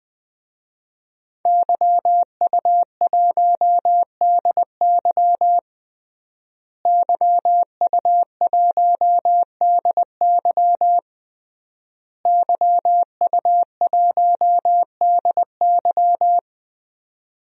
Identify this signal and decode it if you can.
Morse code, heard as a keyed tone, about 20 wpm, 700 Hz. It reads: YU1DY YU1DY YU1DY
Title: DXCC-100 Call Signs - 5 Characters - Repeated 3x - 20wpm